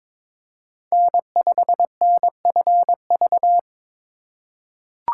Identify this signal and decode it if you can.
Morse code, heard as a keyed tone, about 22 wpm, 700 Hz. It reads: N5NFV
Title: USA Call Signs - 5 Characters - 4x Character Spacing - 22wpm